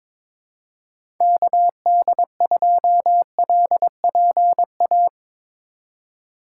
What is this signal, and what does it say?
Morse code, heard as a keyed tone, about 22 wpm, 700 Hz. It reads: KD2LPA
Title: USA Call Signs - 6 Characters - 3x Character Spacing - 22wpm